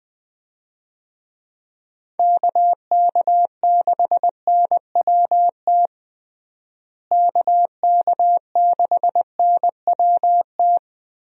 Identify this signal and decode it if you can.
Morse code, heard as a keyed tone, about 21 wpm, 700 Hz. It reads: KK6NWT KK6NWT
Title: USA Call Signs - 6 Characters - Repeated 2x - 20wpm